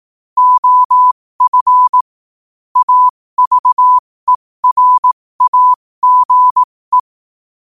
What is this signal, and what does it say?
Morse code, heard as a keyed tone, about 18 wpm, spaced as for 15 wpm, 1000 Hz. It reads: OF AVERAGE
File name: qotdfiles / 15wpm